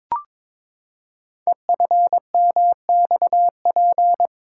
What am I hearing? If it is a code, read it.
Morse code, heard as a keyed tone, about 22 wpm, 700 Hz. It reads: EFMXP
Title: Code Groups - 5 Letters - 22wpm